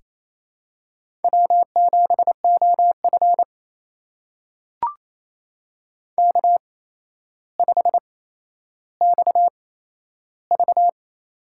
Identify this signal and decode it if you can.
Morse code, heard as a keyed tone, about 28 wpm, spaced as for 12 wpm, 700 Hz. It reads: W7OF K5XV
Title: USA Call Signs - 4 Characters - 8x Character Spacing - 28wpm